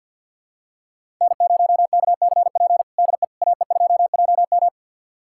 Morse code, heard as a keyed tone, about 52 wpm, 700 Hz. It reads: N0KCP DE AE2JM
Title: USA Callsign DE Callsign 50wpm